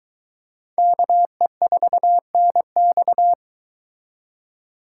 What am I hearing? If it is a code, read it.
Morse code, heard as a keyed tone, about 22 wpm, 700 Hz. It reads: KE4NX